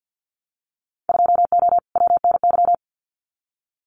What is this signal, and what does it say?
Morse code, heard as a keyed tone, about 50 wpm, 700 Hz. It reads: 2O WNY
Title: Winter Field Day Exchanges Encore 50wpm